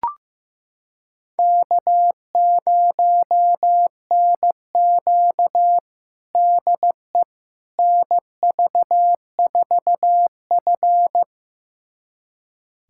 Morse code, heard as a keyed tone, about 15 wpm, 700 Hz. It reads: K0NQ DE NV4F
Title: USA Callsign DE Callsign - 4 Characters - 15wpm